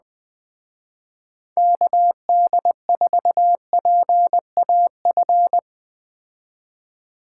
Morse code, heard as a keyed tone, about 20 wpm, 700 Hz. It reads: KD4PAF
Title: Speed Racing - USA 2x3 Call Signs 17wpm